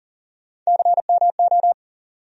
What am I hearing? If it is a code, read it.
Morse code, heard as a keyed tone, about 40 wpm, 700 Hz. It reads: CMO
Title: USA Call Sign Postfixes - 3 Letters 40wpm